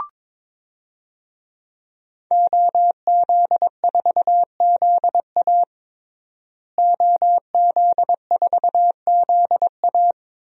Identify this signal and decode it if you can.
Morse code, heard as a keyed tone, about 21 wpm, 700 Hz. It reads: OZ4ZA OZ4ZA
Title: DXCC-100 Call Signs - 5 Characters - Repeated 2x - 22wpm